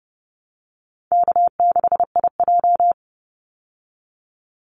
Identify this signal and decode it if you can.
Morse code, heard as a keyed tone, about 30 wpm, 700 Hz. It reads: K6IJ
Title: Easy Call Signs Repeated 3x - 30wpm